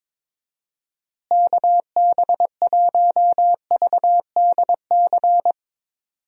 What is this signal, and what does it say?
Morse code, heard as a keyed tone, about 22 wpm, 700 Hz. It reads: KB1VDC
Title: USA Call Signs - 6 Characters - Repeated 3x - 22wpm